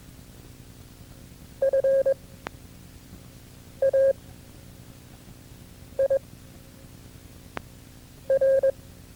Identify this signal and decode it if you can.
Morse code, heard as a keyed tone, about 22 wpm, 540 Hz. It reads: F A I R
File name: FreeFiles / CW-practice / KB6MT-tape2-side1